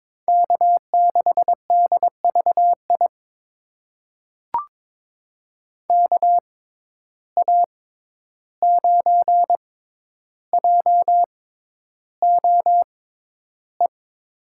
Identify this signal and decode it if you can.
Morse code, heard as a keyed tone, about 22 wpm, spaced as for 11 wpm, 700 Hz. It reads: K6DVI KA9JOE
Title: USA Call Signs - 6x Character Spacing - 22wpm